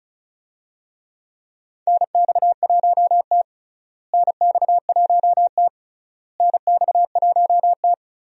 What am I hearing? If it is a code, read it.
Morse code, heard as a keyed tone, about 34 wpm, 700 Hz. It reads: NX1T NX1T NX1T
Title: USA Call Signs - 4 Characters - Repeated 3x - 35wpm